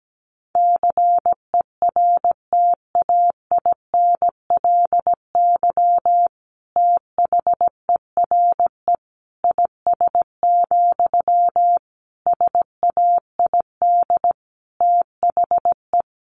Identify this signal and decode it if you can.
Morse code, heard as a keyed tone, about 17 wpm, 700 Hz. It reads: CERTAINLY THERE IS, SAID THE
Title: Sentences from Top 1100 Words 17wpm